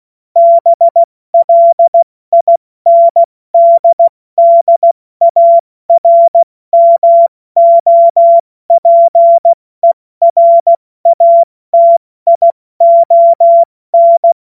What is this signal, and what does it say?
Morse code, heard as a keyed tone, about 16 wpm, 680 Hz. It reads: BLINDDARMOPERATION
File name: Morsekurs / Witze / Witz 20. Juni 2023 70 F80